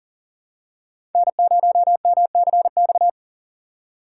Morse code, heard as a keyed tone, about 40 wpm, 700 Hz. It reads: N0MCX
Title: USA Call Signs - 5 Characters - 4x Character Spacing - 40wpm